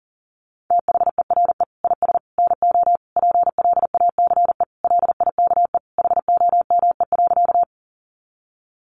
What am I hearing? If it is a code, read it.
Morse code, heard as a keyed tone, about 40 wpm, 700 Hz. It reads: THERE IS NO PLACE LIKE HOME.
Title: Sentences from Top 200 Words 40wpm